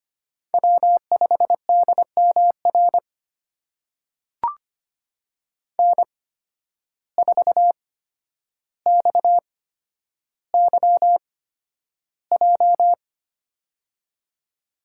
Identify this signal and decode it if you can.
Morse code, heard as a keyed tone, about 25 wpm, spaced as for 11 wpm, 700 Hz. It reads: W5DMR N4XYJ
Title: USA Call Signs - 5 Characters - 8x Character Spacing - 25wpm